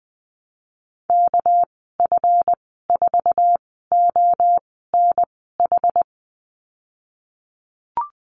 Morse code, heard as a keyed tone, about 19 wpm, spaced as for 14 wpm, 700 Hz. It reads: KF4ONH
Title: 219 Lesson 35g - Number 0 - Call Signs 20wpm